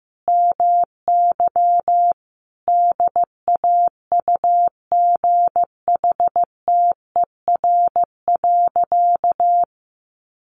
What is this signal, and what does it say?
Morse code, heard as a keyed tone, about 15 wpm, 700 Hz. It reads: MY DAUGHTER.
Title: Sentences from Top 1500 Words 15wpm